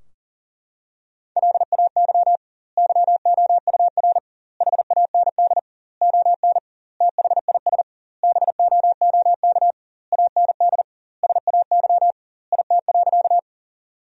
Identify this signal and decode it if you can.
Morse code, heard as a keyed tone, about 37 wpm, 700 Hz. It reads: LAY YOUR HAND ON THIS BOOK AND SAY IT.
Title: 2x Word-Spacing - Sentences from Top 400 Words - 40wpm